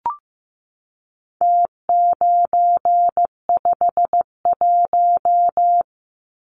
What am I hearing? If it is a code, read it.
Morse code, heard as a keyed tone, about 15 wpm, 700 Hz. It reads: T951
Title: Numbers - Contest Style - 4-Digit 15wpm